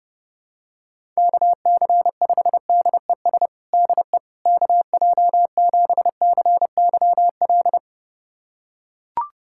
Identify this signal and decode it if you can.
Morse code, heard as a keyed tone, about 32 wpm, 700 Hz. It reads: KC5DES DE KJ7CYL
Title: USA Callsign DE Callsign - 6 Characters - 30wpm